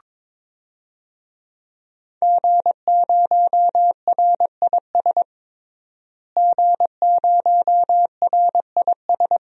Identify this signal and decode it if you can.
Morse code, heard as a keyed tone, about 21 wpm, 700 Hz. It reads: G0RIS G0RIS
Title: DXCC-100 Call Signs - 5 Characters - Repeated 3x - 22wpm